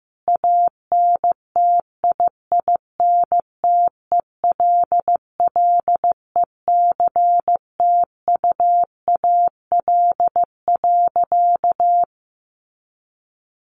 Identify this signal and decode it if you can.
Morse code, heard as a keyed tone, about 14 wpm, 700 Hz. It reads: ANTIINTELLECTUAL.
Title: A Sound Of Thunder 15wpm